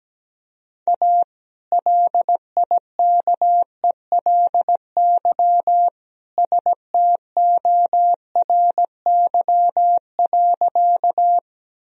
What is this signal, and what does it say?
Morse code, heard as a keyed tone, about 17 wpm, 700 Hz. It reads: A LIKELY STORY.
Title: Sentences from Top 1000 Words - Review Part 4 - 17wpm